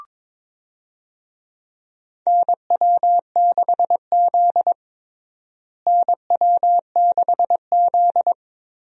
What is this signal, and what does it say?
Morse code, heard as a keyed tone, about 22 wpm, 700 Hz. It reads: NW6Z NW6Z
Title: USA Call Signs - 4 Characters - Repeated 3x - 22wpm